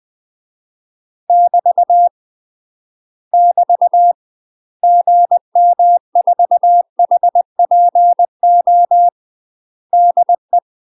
Morse code, heard as a keyed tone, about 20 wpm, 700 Hz. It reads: <BT> <BT> GM4HPO DE
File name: UK MASTER QSOs TRAINING / 20WPM / UK MASTER QSOs 0050